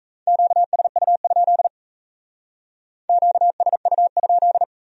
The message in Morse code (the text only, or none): QSU? QSU?